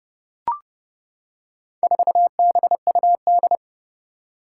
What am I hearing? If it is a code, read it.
Morse code, heard as a keyed tone, about 30 wpm, 700 Hz. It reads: 4BUD
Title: Four Challenging Letter-Number Combinations 30wpm